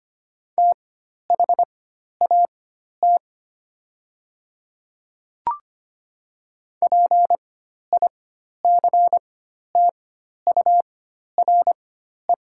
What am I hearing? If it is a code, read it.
Morse code, heard as a keyed tone, about 25 wpm, spaced as for 12 wpm, 700 Hz. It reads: THAT PICTURE
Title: 4x Character Spacing - Top 200 Words - 25wpm